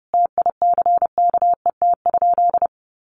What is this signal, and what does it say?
Morse code, heard as a keyed tone, about 30 wpm, 700 Hz. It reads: TICKET?
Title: Sentences from Top 1400 Words 30wpm